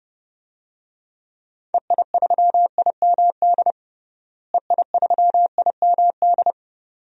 Morse code, heard as a keyed tone, about 31 wpm, 700 Hz. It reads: EI3IMD EI3IMD
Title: DXCC-100 Call Signs - 6 Characters - Repeated 2x - 30wpm